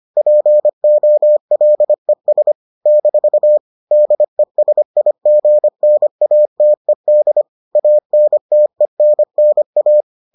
Morse code, heard as a keyed tone, about 25 wpm, 600 Hz. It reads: POLES - DESIGNATED ANTENNA